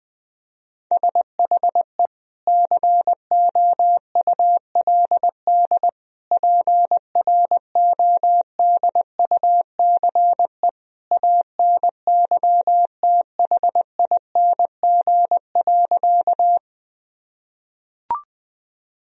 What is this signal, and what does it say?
Morse code, heard as a keyed tone, about 20 wpm, 700 Hz. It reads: SHE COULD PRODUCE ANYTHING.
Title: Sentences from Top 1000 Words - Review Part 3 - 20wpm